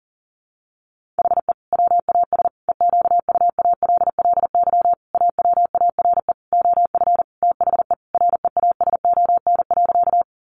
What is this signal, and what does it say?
Morse code, heard as a keyed tone, about 40 wpm, 700 Hz. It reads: HE WAS EQUALLY AWARE OF THE REASON.
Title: Sentences from Top 1400 Words 40wpm